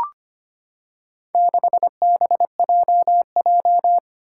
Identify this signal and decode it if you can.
Morse code, heard as a keyed tone, about 25 wpm, 700 Hz. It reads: 6BJJ
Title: Four Challenging Letter-Number Combinations 25wpm